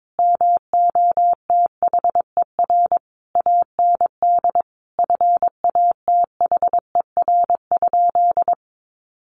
Morse code, heard as a keyed tone, about 22 wpm, 700 Hz. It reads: MOTHER AND FATHER?
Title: Sentences from Top 1200 Words 22wpm